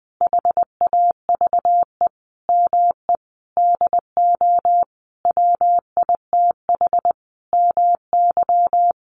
Morse code, heard as a keyed tone, about 20 wpm, 700 Hz. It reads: HAVE ME DO WITH MY
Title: Sentences from Top 300 Words 20wpm